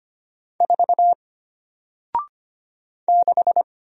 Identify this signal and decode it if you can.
Morse code, heard as a keyed tone, about 25 wpm, 700 Hz. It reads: 4 6
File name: Single-Letter-Number-25wpm